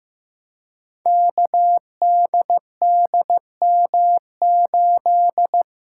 Morse code, heard as a keyed tone, about 15 wpm, 700 Hz. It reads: KDDM8
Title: Code Groups - 5 Letter:Numbers - 15wpm